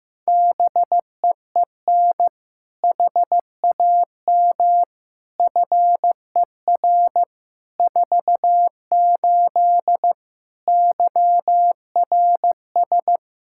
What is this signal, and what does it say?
Morse code, heard as a keyed tone, about 15 wpm, 700 Hz. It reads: BEEN HAM FER 48 YRS
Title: Practice QSO - Exchange 3 - More Info - 15wpm